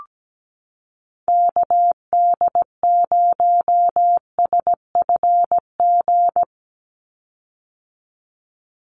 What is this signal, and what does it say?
Morse code, heard as a keyed tone, about 17 wpm, 700 Hz. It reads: KD0SFG
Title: USA Call Signs 17wpm